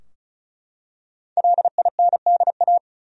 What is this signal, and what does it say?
Morse code, heard as a keyed tone, about 36 wpm, 700 Hz. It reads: LINDA
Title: USA Common Names - 35wpm